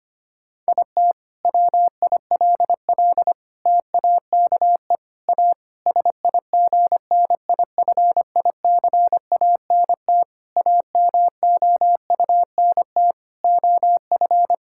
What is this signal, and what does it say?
Morse code, heard as a keyed tone, about 25 wpm, 700 Hz. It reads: IT WILL TAKE A SIGNIFICANT AMOUNT OF
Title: Sentences from Top 1000 Words - Review Part 6 - 25wpm